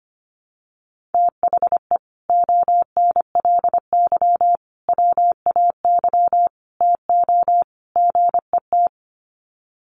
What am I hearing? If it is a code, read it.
Morse code, heard as a keyed tone, about 24 wpm, 700 Hz. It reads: THE ONLY WAY TO GET
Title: Sets of 5 Words Encore2 25wpm